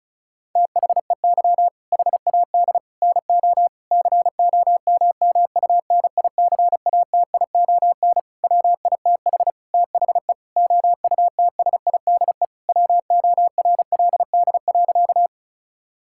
Morse code, heard as a keyed tone, about 36 wpm, 700 Hz. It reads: THEY HAD NO COMMUNICATION WITH THE OUTSIDE WORLD.
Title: Sentences from Top 1000 Words - Review Part 6 - 35wpm